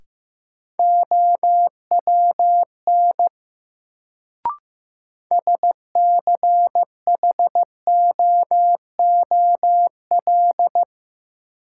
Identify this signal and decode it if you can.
Morse code, heard as a keyed tone, about 16 wpm, 700 Hz. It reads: OWN SCHOOL
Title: Sentences from Top 300 Words - New Word Drill - 15wpm